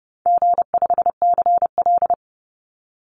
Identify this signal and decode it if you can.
Morse code, heard as a keyed tone, about 30 wpm, 700 Hz. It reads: G5CL
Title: Easy Call Signs Repeated 2x - 30wpm